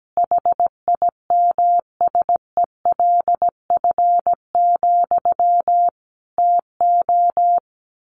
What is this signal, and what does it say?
Morse code, heard as a keyed tone, about 17 wpm, 700 Hz. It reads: HIMSELF, TO
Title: A Sound Of Thunder 17wpm